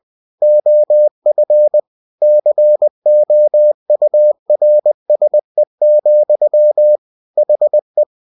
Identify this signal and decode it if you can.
Morse code, heard as a keyed tone, about 20 wpm, 600 Hz. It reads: OF COURSE, HE
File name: cw-books / winnie-the-pooh / winnie-20wpm-page-0037